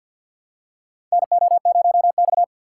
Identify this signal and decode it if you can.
Morse code, heard as a keyed tone, about 47 wpm, 700 Hz. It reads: NO0X
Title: USA Call Signs - 4 Characters - 6x Character Spacing - 50wpm